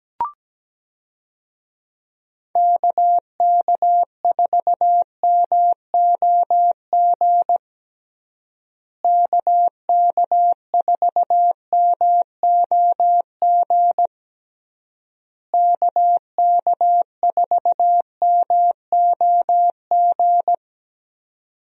Morse code, heard as a keyed tone, about 17 wpm, 700 Hz. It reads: KK4MOG KK4MOG KK4MOG
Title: USA Call Signs - 6 Characters - Repeated 3x - 17wpm